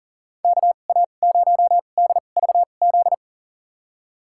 Morse code, heard as a keyed tone, about 40 wpm, spaced as for 29 wpm, 700 Hz. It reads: KA0DVZ
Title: USA Call Signs - 6 Characters - 2x Character Spacing - 40wpm